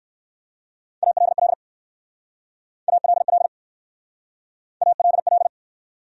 Morse code, heard as a keyed tone, about 59 wpm, 700 Hz. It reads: ALL ALL ALL